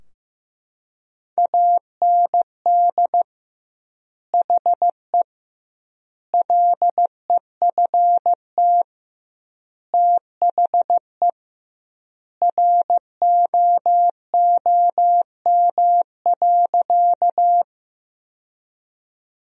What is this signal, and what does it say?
Morse code, heard as a keyed tone, about 16 wpm, 700 Hz. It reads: AND HE LEFT THE ROOM.
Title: 2x Word-Spacing - Sentences from Top 300 Words - 15wpm